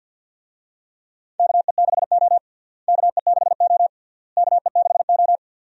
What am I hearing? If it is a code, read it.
Morse code, heard as a keyed tone, about 52 wpm, 700 Hz. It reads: KE6O KE6O KE6O
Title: USA Call Signs - 4 Characters - Repeated 3x - 50wpm